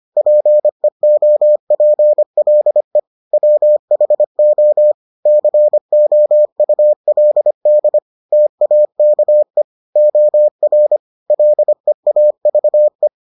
Morse code, heard as a keyed tone, about 25 wpm, 600 Hz. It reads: PEOPLE WHO COULD TAKE OR LEAVE